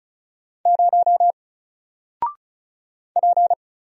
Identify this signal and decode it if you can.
Morse code, heard as a keyed tone, about 36 wpm, 700 Hz. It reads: 0 P
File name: Single-Letter-Number-35wpm